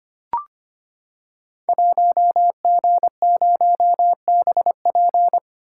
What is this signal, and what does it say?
Morse code, heard as a keyed tone, about 25 wpm, 700 Hz. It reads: 1G0BP
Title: Code Groups - 5 Letter:Numbers - 25wpm